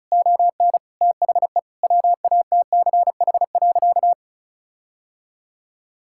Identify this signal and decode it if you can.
Morse code, heard as a keyed tone, about 34 wpm, 700 Hz. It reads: ON THE WATCH.
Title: Sentences from Top 1600 Words 35wpm